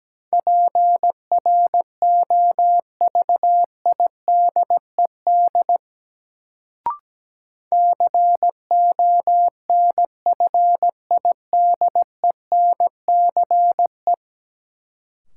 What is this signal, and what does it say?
Morse code, heard as a keyed tone, about 17 wpm, 700 Hz. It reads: PROVIDED CONFIDENCE
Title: Sentences from Top 1500 Words - New Word Drill - 17wpm